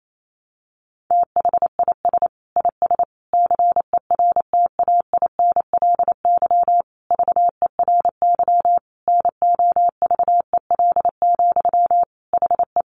Morse code, heard as a keyed tone, about 28 wpm, 700 Hz. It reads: THIS IS CERTAINLY VERY NOVEL, HE
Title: Sentences from Top 1300 Words 28wpm